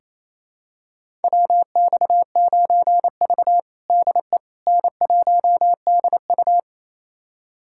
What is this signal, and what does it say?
Morse code, heard as a keyed tone, about 28 wpm, 700 Hz. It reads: WX9V DE N1DU
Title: USA Callsign DE Callsign - 4 Characters - 28wpm